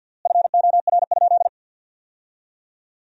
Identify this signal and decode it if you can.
Morse code, heard as a keyed tone, about 53 wpm, 700 Hz. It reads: FOR?
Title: 3x Word-Spacing - Sentences from Top 100 Words - 50wpm